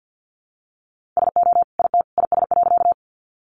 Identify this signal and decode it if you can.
Morse code, heard as a keyed tone, about 50 wpm, 700 Hz. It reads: SO IT IS.
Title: Sentences from Top 100 Words 50wpm